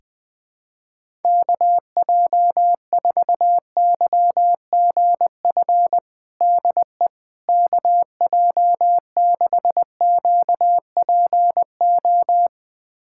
Morse code, heard as a keyed tone, about 21 wpm, 700 Hz. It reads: KJ4YGF DE KJ6QPO
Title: USA Callsign DE Callsign 20wpm